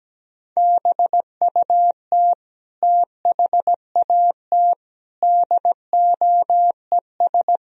Morse code, heard as a keyed tone, about 17 wpm, 700 Hz. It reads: BUT THAT DOES
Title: Speed-Racing - Sets of 5 Words 17wpm